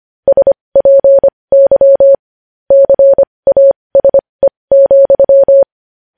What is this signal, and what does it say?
Morse code, heard as a keyed tone, about 25 wpm, spaced as for 20 wpm, 550 Hz. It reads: SPY CASE,
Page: news-headlines-in-morse-code-at-20-wpm